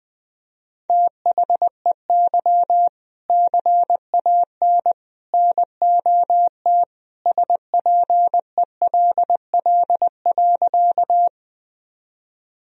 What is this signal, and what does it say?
Morse code, heard as a keyed tone, about 21 wpm, 700 Hz. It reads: THEY CAN NOT SPELL.
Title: Sentences from Top 1000 Words - Review Part 1 - 20wpm